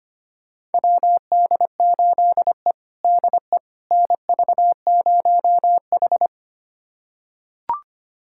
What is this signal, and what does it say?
Morse code, heard as a keyed tone, about 25 wpm, 700 Hz. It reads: WD8E DE NV0H
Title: USA Callsign DE Callsign - 4 Characters - 25wpm